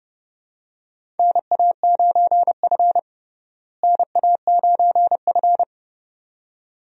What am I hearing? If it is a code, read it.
Morse code, heard as a keyed tone, about 31 wpm, 700 Hz. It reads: NA9F NA9F
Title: USA Call Signs - 4 Characters - Repeated 2x - 30wpm